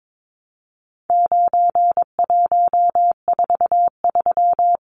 Morse code, heard as a keyed tone, about 22 wpm, 700 Hz. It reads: 9143
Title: Numbers - 4-Digit 22wpm